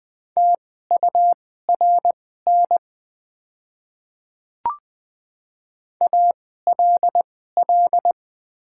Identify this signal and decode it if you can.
Morse code, heard as a keyed tone, about 20 wpm, spaced as for 14 wpm, 700 Hz. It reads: TURN ALL
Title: 2x Character Spacing - Top 200 Words - 20wpm